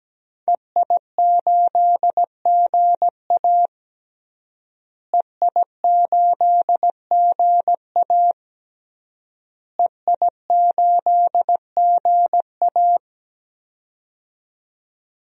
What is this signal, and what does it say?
Morse code, heard as a keyed tone, about 17 wpm, 700 Hz. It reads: EI8GA EI8GA EI8GA
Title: DXCC-100 Call Signs - 5 Characters - Repeated 3x - 17wpm